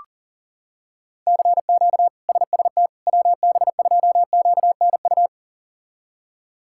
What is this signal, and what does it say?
Morse code, heard as a keyed tone, about 40 wpm, 700 Hz. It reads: CQ SST WB2QNU
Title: K1USN SST QSO Practice - 40wpm